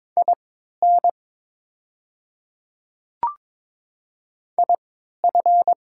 Morse code, heard as a keyed tone, about 22 wpm, spaced as for 13 wpm, 700 Hz. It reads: IN IF
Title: 3x Character Spacing - Top 200 Words - 22wpm